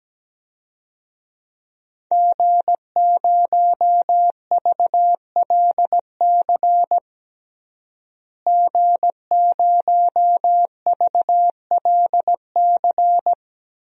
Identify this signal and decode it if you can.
Morse code, heard as a keyed tone, about 17 wpm, 700 Hz. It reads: G0VLC G0VLC
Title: DXCC-100 Call Signs - 5 Characters - Repeated 3x - 17wpm